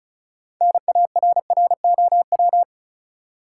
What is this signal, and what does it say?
Morse code, heard as a keyed tone, about 36 wpm, 700 Hz. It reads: NARROW
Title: Six Letter Words - 35wpm